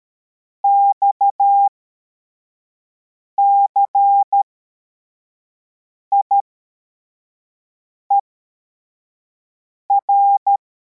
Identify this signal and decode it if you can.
Morse code, heard as a keyed tone, about 13 wpm, spaced as for 4 wpm, 800 Hz. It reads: XCIER